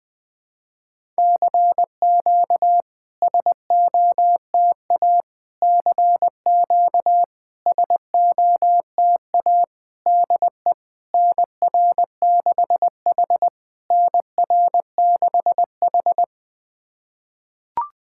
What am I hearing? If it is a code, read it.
Morse code, heard as a keyed tone, about 20 wpm, 690 Hz. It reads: CQ SOTA CQ SOTA DE NR6H NR6H
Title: SOTA QSO Practice 20wpm